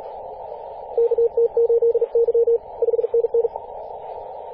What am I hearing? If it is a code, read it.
Morse code, heard as a keyed tone, about 39 wpm, 470 Hz, with radio noise. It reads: KT8Y 5NN
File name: posts / 2016-apr-22-2153z-kh8-kc0w-18-080mhz-cw